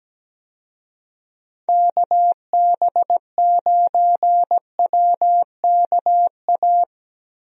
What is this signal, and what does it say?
Morse code, heard as a keyed tone, about 17 wpm, 700 Hz. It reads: KB9WKA
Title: USA Call Signs - 6 Characters - Repeated 3x - 17wpm